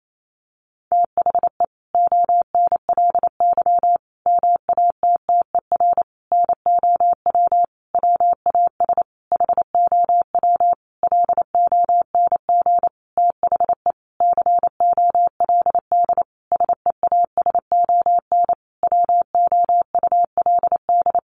Morse code, heard as a keyed tone, about 28 wpm, 700 Hz. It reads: THE ONLY MATTER NOW WAS HOW LONG THE COLD SEASON WOULD
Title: Sentences from Top 1400 Words 28wpm